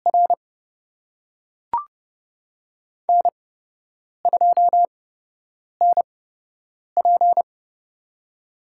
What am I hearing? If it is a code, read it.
Morse code, heard as a keyed tone, about 30 wpm, spaced as for 8 wpm, 700 Hz. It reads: R N2NP
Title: USA Call Signs - 4 Characters - 8x Character Spacing - 30wpm